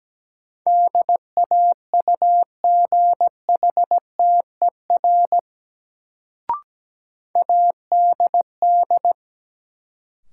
Morse code, heard as a keyed tone, about 17 wpm, 700 Hz. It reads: DAUGHTER ADD
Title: Sentences from Top 1500 Words - New Word Drill Alt - 17wpm